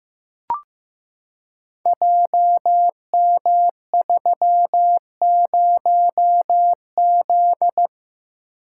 Morse code, heard as a keyed tone, about 15 wpm, 700 Hz. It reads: JM30Z
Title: Code Groups - 5 Letter:Numbers - 15wpm